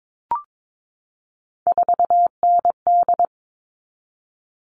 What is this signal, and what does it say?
Morse code, heard as a keyed tone, about 22 wpm, 700 Hz. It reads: 4ND
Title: All Cut Numbers - 3-Digit 22wpm